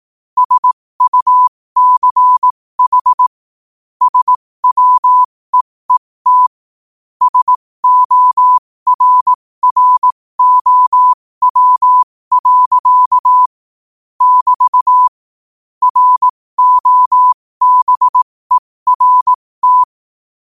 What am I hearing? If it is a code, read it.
Morse code, heard as a keyed tone, about 18 wpm, spaced as for 15 wpm, 1000 Hz. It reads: SUCH SWEET SORROW. <BT> ROBERT
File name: qotdfiles / 15wpm